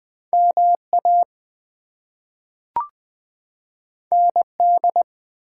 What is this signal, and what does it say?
Morse code, heard as a keyed tone, about 20 wpm, 700 Hz. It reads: MA ND
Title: US States and Canadian Provinces 20wpm